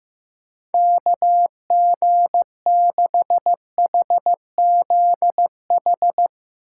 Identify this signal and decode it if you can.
Morse code, heard as a keyed tone, about 15 wpm, 700 Hz. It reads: KG6HZH
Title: USA Call Signs - 6 Characters - 2x Character Spacing - 15wpm